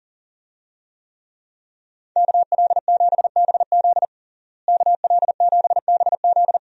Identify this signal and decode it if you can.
Morse code, heard as a keyed tone, about 37 wpm, 700 Hz. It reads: KL7BZ KL7BZ
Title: DXCC-100 Call Signs - 5 Characters - Repeated 2x - 40wpm